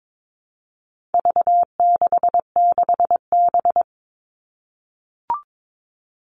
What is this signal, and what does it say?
Morse code, heard as a keyed tone, about 22 wpm, 700 Hz. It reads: V66B
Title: All Cut Numbers - 4-Digit 22wpm